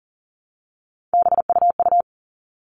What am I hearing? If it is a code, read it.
Morse code, heard as a keyed tone, about 40 wpm, 700 Hz. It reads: BUU
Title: All Cut Numbers - 3-Digit 40wpm